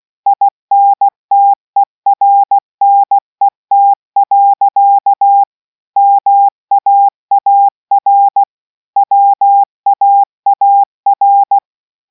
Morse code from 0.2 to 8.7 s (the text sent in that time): INTERNET. MAAR